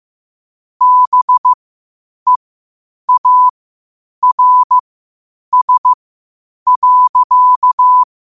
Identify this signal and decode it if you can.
Morse code, heard as a keyed tone, about 15 wpm, spaced as for 8 wpm, 1000 Hz. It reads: BEARS.
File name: qotdfiles / 8wpm